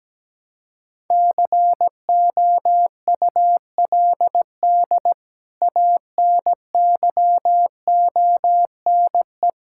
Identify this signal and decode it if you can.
Morse code, heard as a keyed tone, about 17 wpm, 700 Hz. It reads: COULD ANYONE
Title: Sentences from Top 1000 Words - Review Part 5 - 17wpm